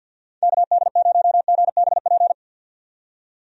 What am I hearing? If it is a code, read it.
Morse code, heard as a keyed tone, about 48 wpm, 700 Hz. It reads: KD0GBP
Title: USA Call Signs - 6 Characters - 50wpm